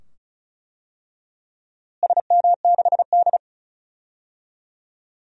Morse code, heard as a keyed tone, about 34 wpm, 700 Hz. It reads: SM6D
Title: DXCC-100 Call Signs - 4 Characters - Repeated 2x - 35wpm